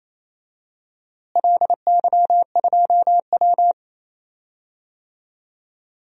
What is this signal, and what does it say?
Morse code, heard as a keyed tone, about 27 wpm, 700 Hz. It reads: LY2W
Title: DXCC-100 Call Signs - 4 Characters - Repeated 2x - 28wpm